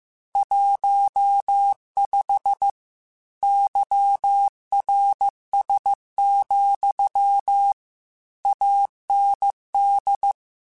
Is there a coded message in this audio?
Yes: Morse code from 0.4 to 10.3 s, heard as a keyed tone, about 15 wpm, 800 Hz. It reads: 15 YRS, AND